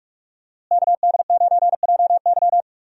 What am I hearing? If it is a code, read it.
Morse code, heard as a keyed tone, about 45 wpm, 700 Hz. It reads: KD9JY
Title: USA Call Signs - 5 Characters - Repeated 2x - 45wpm